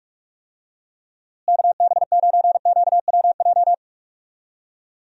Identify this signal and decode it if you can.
Morse code, heard as a keyed tone, about 46 wpm, 700 Hz. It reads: KB9QWJ
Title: USA Call Signs - 6 Characters - Repeated 2x - 45wpm